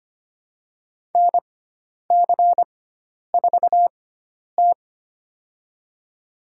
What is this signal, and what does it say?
Morse code, heard as a keyed tone, about 25 wpm, spaced as for 10 wpm, 700 Hz. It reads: NC4T
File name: Lesson 16g - Letter c - Call Signs 25wpm